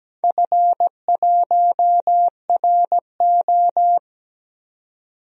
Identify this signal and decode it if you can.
Morse code, heard as a keyed tone, about 17 wpm, 700 Hz. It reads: F1RO
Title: DXCC-100 Call Signs - 4 Characters - 17wpm